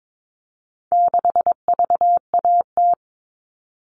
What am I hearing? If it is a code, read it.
Morse code, heard as a keyed tone, about 22 wpm, 700 Hz. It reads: 6VAT
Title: All Cut Numbers - 4-Digit 22wpm